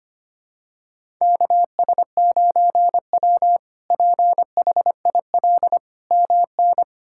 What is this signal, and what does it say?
Morse code, heard as a keyed tone, about 25 wpm, 700 Hz. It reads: KS9W PHIL MN
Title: K1USN SST QSO Practice - 25wpm